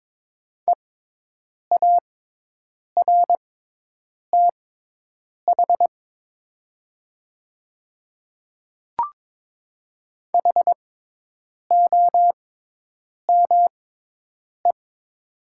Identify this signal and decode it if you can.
Morse code, heard as a keyed tone, about 22 wpm, spaced as for 8 wpm, 700 Hz. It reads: EARTH HOME
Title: 6x Character Spacing - Top 200 Words - 22wpm